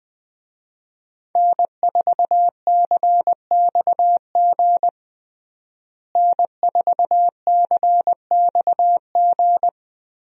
Morse code, heard as a keyed tone, about 20 wpm, 700 Hz. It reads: N4CXG N4CXG
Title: USA Call Signs - 5 Characters - Repeated 3x - 20wpm